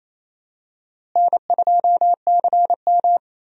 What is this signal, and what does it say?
Morse code, heard as a keyed tone, about 28 wpm, 700 Hz. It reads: N2CM
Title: USA Call Signs - 4 Characters - Repeated 3x - 28wpm